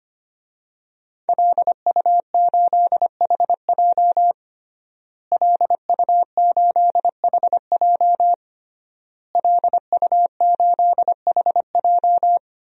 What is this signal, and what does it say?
Morse code, heard as a keyed tone, about 25 wpm, 700 Hz. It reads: LU8HJ LU8HJ LU8HJ
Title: DXCC-100 Call Signs - 5 Characters - Repeated 3x - 25wpm